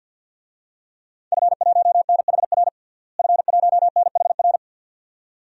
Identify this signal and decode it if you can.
Morse code, heard as a keyed tone, about 46 wpm, 700 Hz. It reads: F1NHR F1NHR
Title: DXCC-100 Call Signs - 5 Characters - Repeated 2x - 50wpm